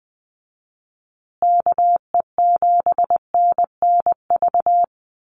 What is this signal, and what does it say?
Morse code, heard as a keyed tone, about 20 wpm, 700 Hz. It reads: KE7NNV
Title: USA Call Signs Repeated 2x 20wpm